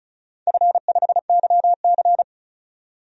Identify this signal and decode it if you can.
Morse code, heard as a keyed tone, about 36 wpm, 700 Hz. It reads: F5YC
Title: DXCC-100 Call Signs - 4 Characters - 35wpm